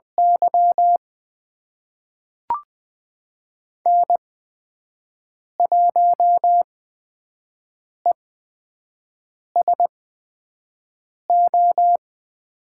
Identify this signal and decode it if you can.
Morse code, heard as a keyed tone, about 20 wpm, spaced as for 5 wpm, 700 Hz. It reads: Y N1ESO
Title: USA Call Signs - 5 Characters - 8x Character Spacing - 20wpm